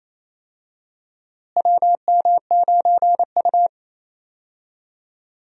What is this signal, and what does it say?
Morse code, heard as a keyed tone, about 27 wpm, 700 Hz. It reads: WM9U
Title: USA Call Signs - 4 Characters - Repeated 2x - 28wpm